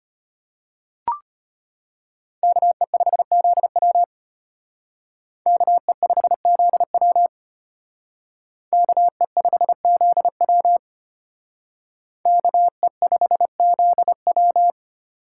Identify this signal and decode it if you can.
Morse code, heard as a keyed tone, about 32 wpm, 700 Hz. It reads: KE5ZW KE5ZW KE5ZW KE5ZW